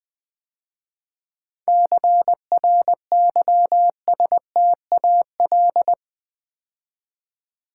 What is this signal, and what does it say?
Morse code, heard as a keyed tone, about 20 wpm, 700 Hz. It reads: CRYSTAL
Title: 7 Letter Words - First 3 Letters Given - 20wpm